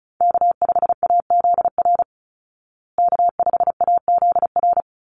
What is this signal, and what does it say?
Morse code, heard as a keyed tone, about 34 wpm, 700 Hz. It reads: K5AZR K5AZR
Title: Call Signs Repeated 2x - 35wpm